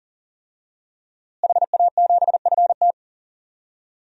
Morse code, heard as a keyed tone, about 40 wpm, 700 Hz. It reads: HA7FT
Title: DXCC-100 Call Signs - 5 Characters - Repeated 3x - 40wpm